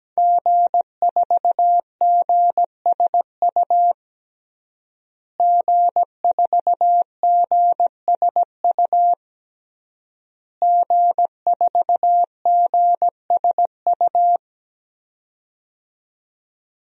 Morse code, heard as a keyed tone, about 17 wpm, 700 Hz. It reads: G4GSU G4GSU G4GSU
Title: DXCC-100 Call Signs - 5 Characters - Repeated 3x - 17wpm